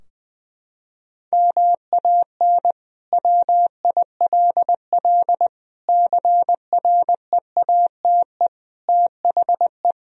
Morse code, heard as a keyed tone, about 20 wpm, 700 Hz. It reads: MAN WILL CREATE THE
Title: Sentences from Top 1000 Words - Review Part 4 - 20wpm